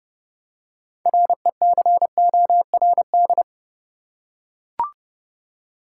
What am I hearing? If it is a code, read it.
Morse code, heard as a keyed tone, about 29 wpm, 700 Hz. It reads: RECORD
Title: Six Letter Words - 30wpm